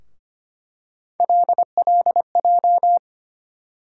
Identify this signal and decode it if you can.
Morse code, heard as a keyed tone, about 25 wpm, 700 Hz. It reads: LLJ
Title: USA Call Sign Postfixes - 3 Letters 25wpm